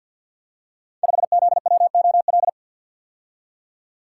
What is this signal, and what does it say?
Morse code, heard as a keyed tone, about 48 wpm, 700 Hz. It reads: 5ZWOL
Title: Code Groups - 5 Letter:Numbers - 50wpm